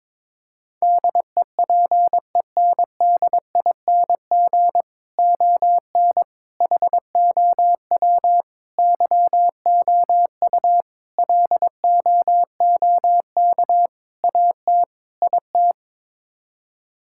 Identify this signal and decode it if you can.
Morse code, heard as a keyed tone, about 22 wpm, 700 Hz. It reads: DEPENDING ON HOW YOU LOOK AT IT
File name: practice / 22wpm / Sets of 7 Words - 22wpm